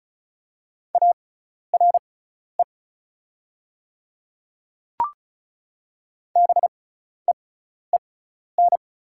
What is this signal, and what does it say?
Morse code, heard as a keyed tone, about 36 wpm, spaced as for 12 wpm, 700 Hz. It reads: ARE BEEN
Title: 6x Character Spacing - Top 100 Words - 35wpm